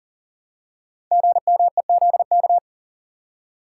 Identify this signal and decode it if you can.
Morse code, heard as a keyed tone, about 40 wpm, 700 Hz. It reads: GMEZK
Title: Code Groups - 5 Letter:Numbers - 40wpm